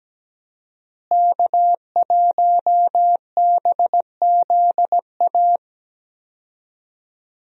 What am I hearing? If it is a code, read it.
Morse code, heard as a keyed tone, about 17 wpm, 700 Hz. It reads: K1BZA
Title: USA Call Signs - 5 Characters - Repeated 3x - 17wpm